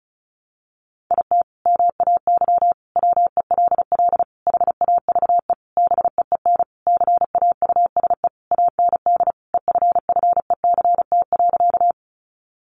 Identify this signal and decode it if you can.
Morse code, heard as a keyed tone, about 34 wpm, 700 Hz. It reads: IT MAY WELL HAVE BEEN CAUSE AND EFFECT.
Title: Sentences from Top 1000 Words 35wpm